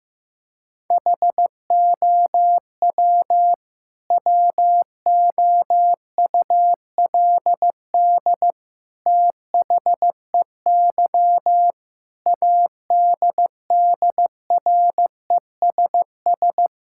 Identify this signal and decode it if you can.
Morse code, heard as a keyed tone, about 15 wpm, 700 Hz. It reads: HOW WOULD THEY ADDRESS
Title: Sentences from Top 1000 Words - Review Part 6 - 15wpm